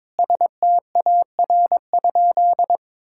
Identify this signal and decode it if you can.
Morse code, heard as a keyed tone, about 22 wpm, 700 Hz. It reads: STAR?
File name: practice / 22wpm / Sentences with 8-Word Phrases - 22wpm